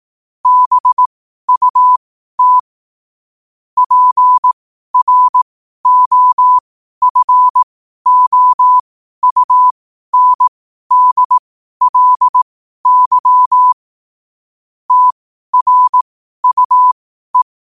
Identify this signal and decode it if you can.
Morse code, heard as a keyed tone, about 18 wpm, spaced as for 12 wpm, 1000 Hz. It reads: BUT PROFOUNDLY TRUE
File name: qotdfiles / 12wpm